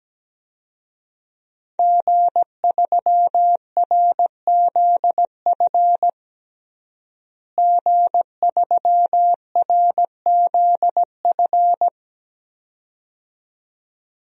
Morse code, heard as a keyed tone, about 17 wpm, 700 Hz. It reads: G3RZF G3RZF
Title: DXCC-100 Call Signs - 5 Characters - Repeated 2x - 17wpm